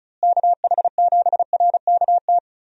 Morse code, heard as a keyed tone, about 36 wpm, 700 Hz. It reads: KH7RKT
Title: USA Call Signs - 6 Characters - Repeated 3x - 35wpm